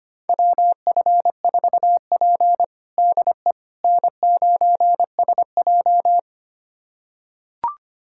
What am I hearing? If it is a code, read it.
Morse code, heard as a keyed tone, about 25 wpm, 700 Hz. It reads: WF4P DE N9SJ
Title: USA Callsign DE Callsign - 4 Characters - 25wpm